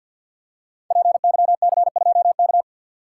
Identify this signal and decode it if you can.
Morse code, heard as a keyed tone, about 53 wpm, 700 Hz. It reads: PYC2K
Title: Code Groups - 5 Letter:Numbers - 50wpm